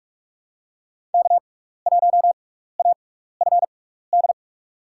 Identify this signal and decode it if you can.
Morse code, heard as a keyed tone, about 46 wpm, spaced as for 16 wpm, 700 Hz. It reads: K1AFD
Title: USA Call Signs - 6x Character Spacing - 45wpm